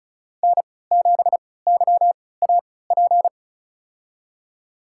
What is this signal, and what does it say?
Morse code, heard as a keyed tone, about 33 wpm, spaced as for 19 wpm, 690 Hz. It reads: N7YAP
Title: USA Call Signs - 5 Characters - 3x Character Spacing - 35wpm